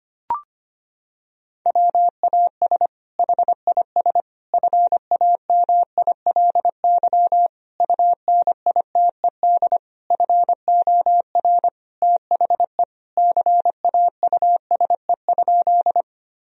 Morse code, heard as a keyed tone, about 25 wpm, 700 Hz. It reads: WAS HIS FAMILY UNITED FOR THE CAUSE?
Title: Sentences from Top 1000 Words - Review Part 4 - 25wpm